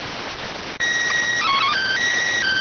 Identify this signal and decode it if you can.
SSTV (slow-scan television), PD 290.